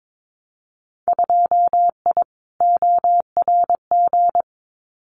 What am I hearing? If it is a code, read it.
Morse code, heard as a keyed tone, about 22 wpm, 700 Hz. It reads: 2I ORG
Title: Winter Field Day Exchanges Encore 22wpm